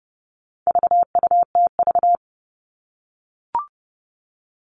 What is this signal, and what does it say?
Morse code, heard as a keyed tone, about 30 wpm, 700 Hz. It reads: VUTV
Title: All Cut Numbers - 4-Digit 30wpm